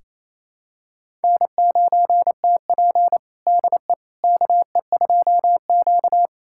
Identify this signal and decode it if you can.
Morse code, heard as a keyed tone, about 27 wpm, 690 Hz. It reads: N9TP DE KE2Q
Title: USA Callsign DE Callsign - 4 Characters - 28wpm